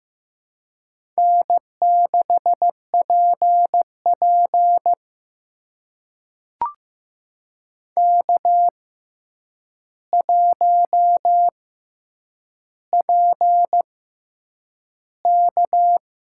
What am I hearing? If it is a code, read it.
Morse code, heard as a keyed tone, about 15 wpm, spaced as for 8 wpm, 700 Hz. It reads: N6PP K1PK
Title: USA Call Signs - 4 Characters - 6x Character Spacing - 15wpm